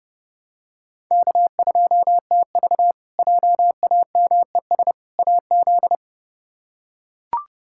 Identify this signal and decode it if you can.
Morse code, heard as a keyed tone, about 30 wpm, 700 Hz. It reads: K2TV JAMES AZ
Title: K1USN SST QSO Practice - 30wpm